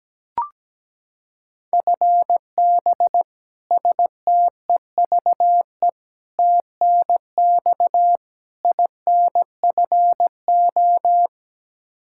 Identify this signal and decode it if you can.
Morse code, heard as a keyed tone, about 17 wpm, 700 Hz. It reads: FB STEVE TNX INFO
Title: Practice QSO - Exchange 3 - More Info - 17wpm